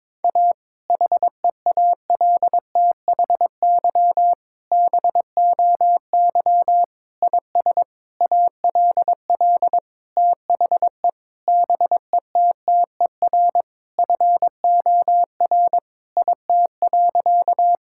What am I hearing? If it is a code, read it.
Morse code, heard as a keyed tone, about 22 wpm, 700 Hz. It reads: A HEALTHY BOY IS ALL THE BETTER FOR IT.
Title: Sentences from Top 1000 Words - Review Part 5 - 22wpm